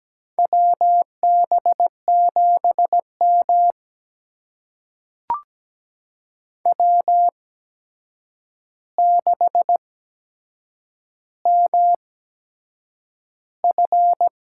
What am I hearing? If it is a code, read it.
Morse code, heard as a keyed tone, about 17 wpm, spaced as for 7 wpm, 700 Hz. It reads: WB7M W6MF
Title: USA Call Signs - 4 Characters - 8x Character Spacing - 17wpm